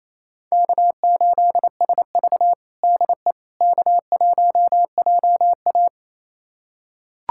Morse code, heard as a keyed tone, about 28 wpm, 700 Hz. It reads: K8SV DE K1JA
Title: USA Callsign DE Callsign - 4 Characters - 28wpm